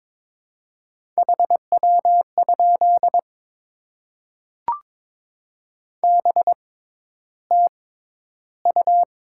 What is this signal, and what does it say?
Morse code, heard as a keyed tone, about 22 wpm, spaced as for 11 wpm, 700 Hz. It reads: HW? BTU
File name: Top 20 QSO Elements - 6x Character Spacing - 22wpm